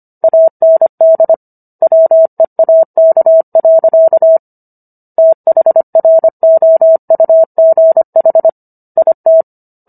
Morse code, heard as a keyed tone, about 25 wpm, 650 Hz. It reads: AND WEAK. THROUGH IT
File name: Book One / 16-Chapter 16 - 25 WPM